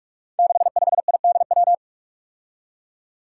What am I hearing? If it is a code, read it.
Morse code, heard as a keyed tone, about 44 wpm, 700 Hz. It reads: 65IDW
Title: Five Challenging Letter-Number Combinations 45wpm